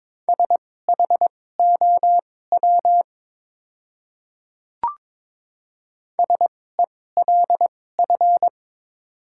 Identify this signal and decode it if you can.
Morse code, heard as a keyed tone, about 22 wpm, spaced as for 16 wpm, 700 Hz. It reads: SHOW SELF
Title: 2x Character Spacing - Top 500 Words - 22wpm